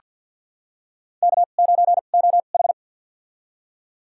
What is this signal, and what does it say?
Morse code, heard as a keyed tone, about 49 wpm, spaced as for 35 wpm, 700 Hz. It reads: K9OH
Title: USA Call Signs - 4 Characters - 2x Character Spacing - 50wpm